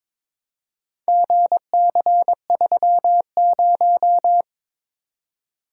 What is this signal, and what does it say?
Morse code, heard as a keyed tone, about 23 wpm, 700 Hz. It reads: GC30
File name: practice / 22wpm / Four Letter-Number Combinations - 22wpm